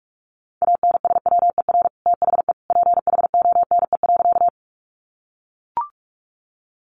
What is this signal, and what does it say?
Morse code, heard as a keyed tone, about 44 wpm, 700 Hz. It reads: ANSWER THE PHONE.
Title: Sentences from Top 1100 Words 45wpm